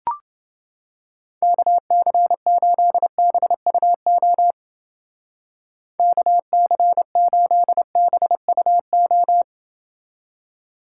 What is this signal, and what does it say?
Morse code, heard as a keyed tone, about 29 wpm, 700 Hz. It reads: KC8BUO KC8BUO